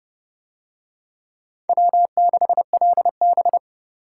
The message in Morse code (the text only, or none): W6LB